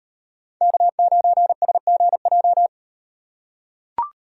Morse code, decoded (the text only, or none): K9SGJ